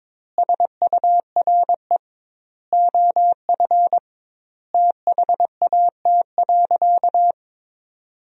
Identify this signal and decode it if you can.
Morse code, heard as a keyed tone, about 22 wpm, 700 Hz. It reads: SURE OF THAT.
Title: 2x Word-Spacing - Sentences from Top 300 Words - 22wpm